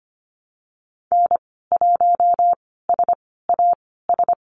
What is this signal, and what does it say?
Morse code, heard as a keyed tone, about 25 wpm, spaced as for 16 wpm, 700 Hz. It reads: N1SAS
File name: Lesson 15g - Number 5 - Call Signs 25wpm